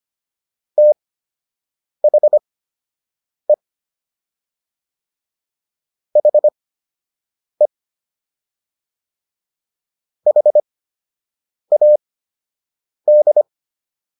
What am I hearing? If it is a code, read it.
Morse code, heard as a keyed tone, about 25 wpm, spaced as for 7 wpm, 600 Hz. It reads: THE HE HAD